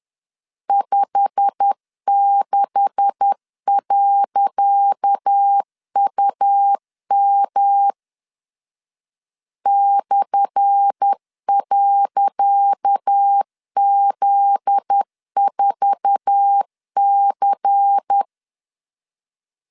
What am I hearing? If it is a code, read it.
Morse code, heard as a keyed tone, about 10 wpm, 800 Hz. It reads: 56.UM /.Z4C
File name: LEZIONI / 10 SIGNS / 10C DOT 10wpm 2940